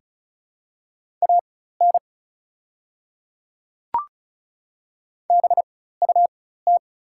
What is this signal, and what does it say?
Morse code, heard as a keyed tone, about 36 wpm, spaced as for 16 wpm, 690 Hz. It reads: AN BUT
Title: 4x Character Spacing - Top 100 Words - 35wpm